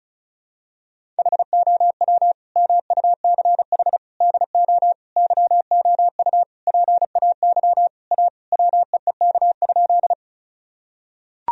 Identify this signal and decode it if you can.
Morse code, heard as a keyed tone, about 36 wpm, 700 Hz. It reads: HOW MUCH DO YOU PAY A WEEK?
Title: Sentences from Top 1000 Words - Review Part 2 - 35wpm